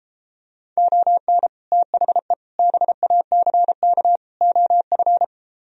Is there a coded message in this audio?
Yes: Morse code from 0.8 to 5.2 s, heard as a keyed tone, about 33 wpm, 700 Hz. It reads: ON THE BACK OF